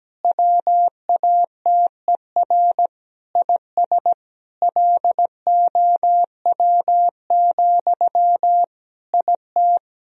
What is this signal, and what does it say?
Morse code, heard as a keyed tone, about 17 wpm, 700 Hz. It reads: WATER IS LOW, IT
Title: Sentences from Top 1000 Words - Review Part 1 - 17wpm